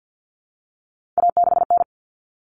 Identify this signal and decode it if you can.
Morse code, heard as a keyed tone, about 50 wpm, 700 Hz. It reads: A6N
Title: All Cut Numbers - 3-Digit 50wpm